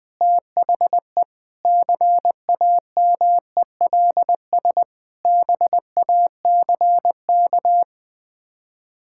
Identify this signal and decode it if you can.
Morse code, heard as a keyed tone, about 20 wpm, 700 Hz. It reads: THE CAMELS BACK
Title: Sets of 8 Words - 20wpm